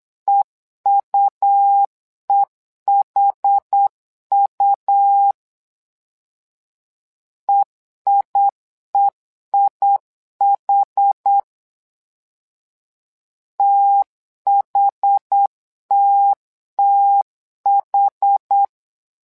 Morse code, heard as a keyed tone, about 8 wpm, 800 Hz. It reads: EUEHU EIEIH THTTH
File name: LEZIONI / 2 UHDN / 2 UH 8wpm 2129